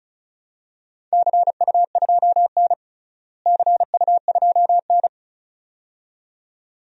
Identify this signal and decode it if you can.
Morse code, heard as a keyed tone, about 36 wpm, 700 Hz. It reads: CU2N CU2N
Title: DXCC-100 Call Signs - 4 Characters - Repeated 2x - 35wpm